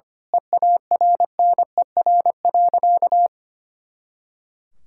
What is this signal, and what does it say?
Morse code, heard as a keyed tone, about 25 wpm, 700 Hz. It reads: EARNER.
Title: Pangram Challenge v2 - 25wpm